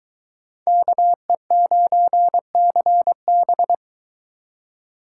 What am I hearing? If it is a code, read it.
Morse code, heard as a keyed tone, about 22 wpm, 700 Hz. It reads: KE9CB